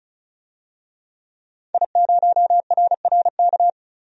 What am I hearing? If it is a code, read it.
Morse code, heard as a keyed tone, about 34 wpm, 700 Hz. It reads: I0RRK
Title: DXCC-100 Call Signs - 5 Characters - 35wpm